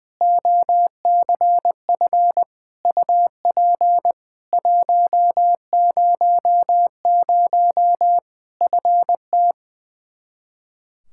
Morse code, heard as a keyed tone, about 21 wpm, 700 Hz. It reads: OCF UP 100 FT
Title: Practice QSO - Exchange 2 - Info - 20wpm